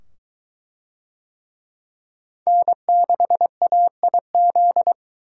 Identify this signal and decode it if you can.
Morse code, heard as a keyed tone, about 22 wpm, 700 Hz. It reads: N6AIZ